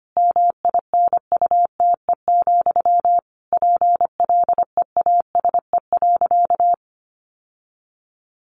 Morse code, heard as a keyed tone, about 25 wpm, 700 Hz. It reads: MINUTE, PLEASE.
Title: Sentences from Top 1500 Words 25wpm